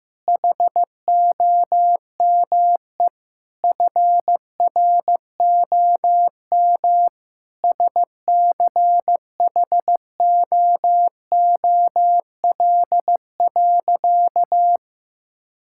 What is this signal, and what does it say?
Morse code, heard as a keyed tone, about 16 wpm, 700 Hz. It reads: HOME FROM SCHOOL.
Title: Sentences from Top 1000 Words - Review Part 6 - 15wpm